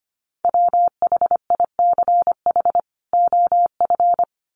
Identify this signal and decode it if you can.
Morse code, heard as a keyed tone, about 25 wpm, 700 Hz. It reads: WHICH OF
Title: Sentences from Top 1300 Words 25wpm